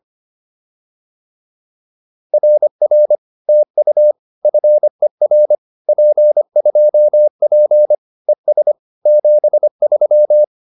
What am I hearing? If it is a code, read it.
Morse code, heard as a keyed tone, about 25 wpm, 600 Hz. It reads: RR TU FER P2P ES 73